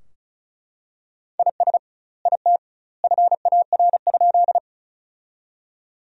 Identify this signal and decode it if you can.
Morse code, heard as a keyed tone, about 36 wpm, 700 Hz. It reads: IS IT FAR?
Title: 2x Word-Spacing - Sentences from Top 300 Words - 35wpm